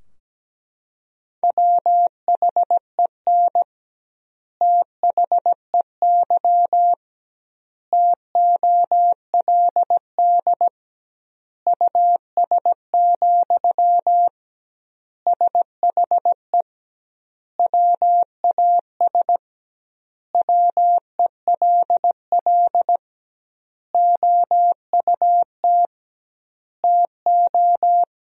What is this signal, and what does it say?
Morse code, heard as a keyed tone, about 17 wpm, 700 Hz. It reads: WHEN THEY TOLD US, SHE WAS WELL OUT TO
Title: 2x Word-Spacing - Sentences from Top 400 Words - 17wpm